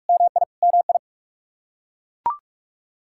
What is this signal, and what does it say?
Morse code, heard as a keyed tone, about 41 wpm, 700 Hz. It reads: MI MI
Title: Practice QSO - Exchange 1 - Beginning - 45wpm